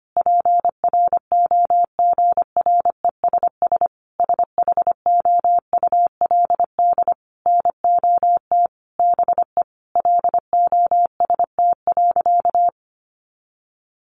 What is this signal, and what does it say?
Morse code, heard as a keyed tone, about 25 wpm, 700 Hz. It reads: PROGRESS SHOULD NOT BE LOST.
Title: Sentences from Top 1400 Words 25wpm